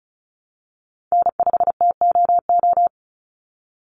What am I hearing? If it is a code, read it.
Morse code, heard as a keyed tone, about 34 wpm, 700 Hz. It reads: N5TOO
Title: Call Signs Repeated 3x - 35wpm